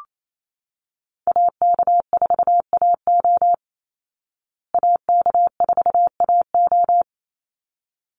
Morse code, heard as a keyed tone, about 28 wpm, 690 Hz. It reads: AK4AO AK4AO
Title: Call Signs Repeated 3x - 28wpm